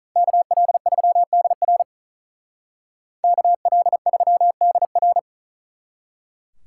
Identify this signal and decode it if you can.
Morse code, heard as a keyed tone, about 46 wpm, 700 Hz. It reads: KL3DR KL3DR